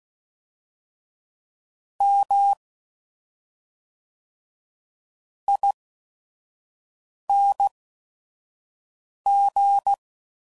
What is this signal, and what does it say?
Morse code, heard as a keyed tone, about 16 wpm, spaced as for 5 wpm, 800 Hz. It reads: M ING
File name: Suffixes / Special ING's / SlowToFaster ING's